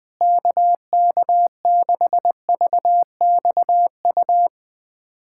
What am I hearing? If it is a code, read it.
Morse code, heard as a keyed tone, about 20 wpm, 700 Hz. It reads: KK6VXU
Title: USA Callsign DE Callsign - 6 Characters - 20wpm